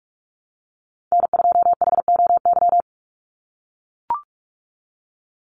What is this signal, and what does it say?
Morse code, heard as a keyed tone, about 44 wpm, 700 Hz. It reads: N2HOY
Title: USA Call Signs 45wpm